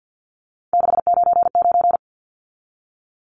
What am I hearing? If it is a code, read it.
Morse code, heard as a keyed tone, about 50 wpm, 700 Hz. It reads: 699
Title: Numbers - 3-Digit 50wpm